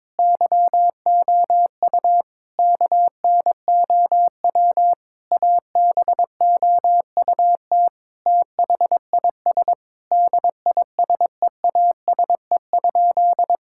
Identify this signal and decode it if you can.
Morse code, heard as a keyed tone, about 21 wpm, 700 Hz. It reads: YOU KNOW ABOUT THIS DISEASE?
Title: Sentences from Top 1000 Words - Review Part 6 - 22wpm